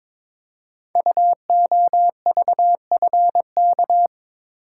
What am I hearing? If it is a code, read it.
Morse code, heard as a keyed tone, about 22 wpm, 700 Hz. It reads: UOVFK
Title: Code Groups - 5 Letters - 22wpm